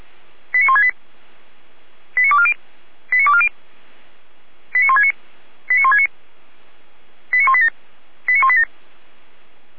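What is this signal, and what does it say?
Five-tone selective call (ZVEI), heard as quick runs of five tones, tones of about 80 ms. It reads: 89168 89240 89240 89169 89169 89187 89187
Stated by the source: ZVEI five-tone call 89167 received by SDR, from the server of Feuerwehr Orlamünde (fire brigade).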